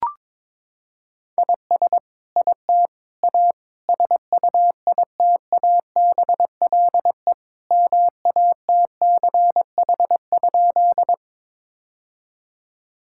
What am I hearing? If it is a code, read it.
Morse code, heard as a keyed tone, about 22 wpm, 700 Hz. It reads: IS IT A SUITABLE MATCH?
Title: Sentences from Top 1600 Words 22wpm